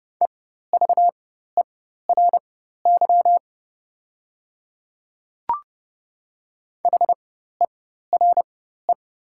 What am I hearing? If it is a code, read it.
Morse code, heard as a keyed tone, about 30 wpm, spaced as for 14 wpm, 700 Hz. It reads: EVERY HERE
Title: 4x Character Spacing - Top 200 Words - 30wpm